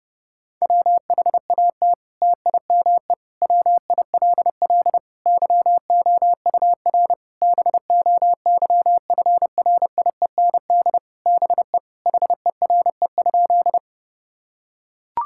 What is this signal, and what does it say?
Morse code, heard as a keyed tone, about 30 wpm, 700 Hz. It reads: WHAT TIME WILL YOUR BOYFRIEND BE HERE?
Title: Sentences from Top 1000 Words - Review Part 5 - 30wpm